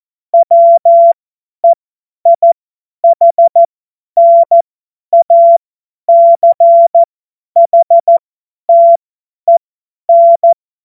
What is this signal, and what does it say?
Morse code, heard as a keyed tone, about 14 wpm, spaced as for 10 wpm, 680 Hz. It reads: WEIHNACHTEN
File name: Morsekurs / Witze / Witz 02. November 2021, 50 F70